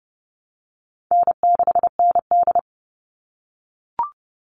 Morse code, heard as a keyed tone, about 30 wpm, 700 Hz. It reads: N6ND
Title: All Cut Numbers - 4-Digit 30wpm